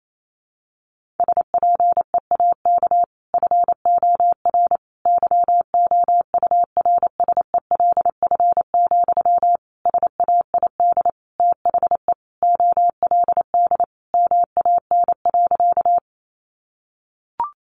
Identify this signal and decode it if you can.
Morse code, heard as a keyed tone, about 28 wpm, 700 Hz. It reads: SPEAK FOR YOURSELF, SAID THE OLD MAN.
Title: Sentences from Top 1100 Words 28wpm